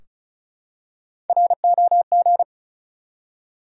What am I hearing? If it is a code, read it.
Morse code, heard as a keyed tone, about 34 wpm, 700 Hz. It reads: ROG
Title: USA Common Names - Max 4 Letters - 35wpm